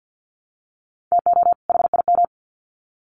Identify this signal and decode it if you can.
Morse code, heard as a keyed tone, about 50 wpm, 700 Hz. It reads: TO HIM
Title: Sets of 2 Words 50wpm